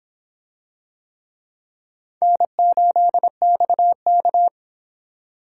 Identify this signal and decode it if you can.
Morse code, heard as a keyed tone, about 26 wpm, 700 Hz. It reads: N8XK